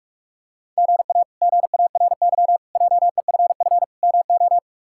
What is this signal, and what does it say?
Morse code, heard as a keyed tone, about 44 wpm, 700 Hz. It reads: GA GARY JEFF MO
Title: K1USN SST QSO Practice - 45wpm